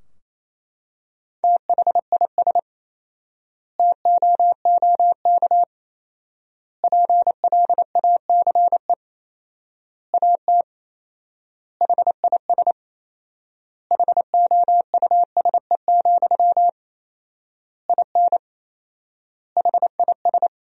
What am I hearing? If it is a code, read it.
Morse code, heard as a keyed tone, about 28 wpm, 700 Hz. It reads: THIS TOOK PLACE AT HIS HOUSE, IN HIS
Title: 4x Word-Spacing - Sentences from Top 300 Words - 28wpm